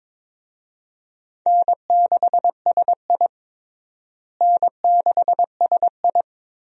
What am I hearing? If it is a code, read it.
Morse code, heard as a keyed tone, about 22 wpm, 700 Hz. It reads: N6SI N6SI
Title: USA Call Signs - 4 Characters - Repeated 3x - 22wpm